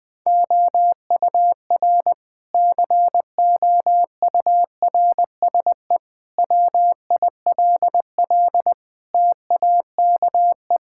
Morse code, heard as a keyed tone, about 20 wpm, 700 Hz. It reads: OUR COURSE WILL TAKE
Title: Sentences from Top 1000 Words - Review Part 6 - 20wpm